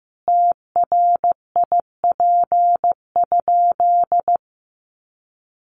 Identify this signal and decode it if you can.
Morse code, heard as a keyed tone, about 15 wpm, 700 Hz. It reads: TRIP?
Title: Sentences from Top 1300 Words 15wpm